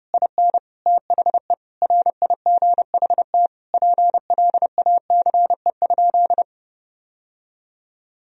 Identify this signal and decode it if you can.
Morse code, heard as a keyed tone, about 30 wpm, 700 Hz. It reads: IN THE RIGHT PLACE?
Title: Sentences from Top 1000 Words - Review Part 2 - 30wpm